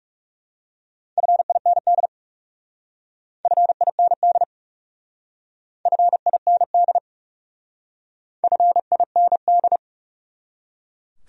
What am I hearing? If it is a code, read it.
Morse code, heard as a keyed tone, about 38 wpm, 700 Hz. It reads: FIND FIND FIND FIND